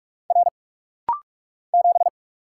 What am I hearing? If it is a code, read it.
Morse code, heard as a keyed tone, about 46 wpm, 700 Hz. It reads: R 7
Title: Single Letter-Number Alt - Rapid-Fire 45wpm